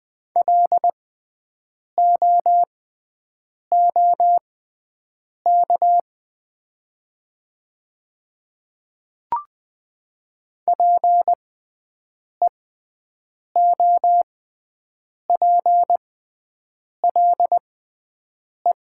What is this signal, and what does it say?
Morse code, heard as a keyed tone, about 20 wpm, spaced as for 7 wpm, 700 Hz. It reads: LOOK PEOPLE
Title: 6x Character Spacing - Top 100 Words - 20wpm